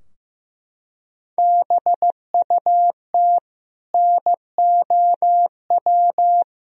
Morse code, heard as a keyed tone, about 15 wpm, 700 Hz. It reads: BUT NOW
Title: Sentences from Top 1000 Words - Review Part 2 - 15wpm